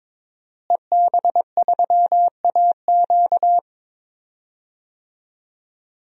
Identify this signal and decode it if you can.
Morse code, heard as a keyed tone, about 22 wpm, 700 Hz. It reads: EB3AQ
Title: DXCC-100 Call Signs - 5 Characters - Repeated 2x - 22wpm